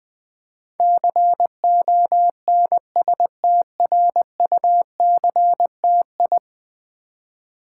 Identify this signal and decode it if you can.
Morse code, heard as a keyed tone, about 20 wpm, 700 Hz. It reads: CONSTRUCTI
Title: Build A Word - 5 Letters At A Time - 12-Letter Words - 20wpm